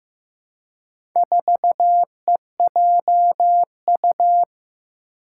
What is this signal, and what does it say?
Morse code, heard as a keyed tone, about 15 wpm, 700 Hz. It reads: 4EJU
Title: Four Challenging Letter-Number Combinations 15wpm